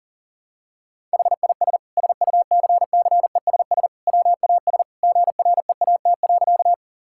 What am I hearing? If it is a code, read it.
Morse code, heard as a keyed tone, about 40 wpm, 700 Hz. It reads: HIS SUCCESS WAS GREAT.
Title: Sentences from Top 1000 Words - Review Part 6 - 40wpm